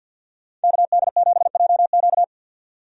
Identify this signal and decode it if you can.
Morse code, heard as a keyed tone, about 51 wpm, 700 Hz. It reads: KD7JQ
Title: USA Call Signs - 5 Characters - 50wpm